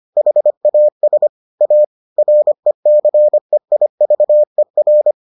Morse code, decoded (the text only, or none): HAS A RECEIVER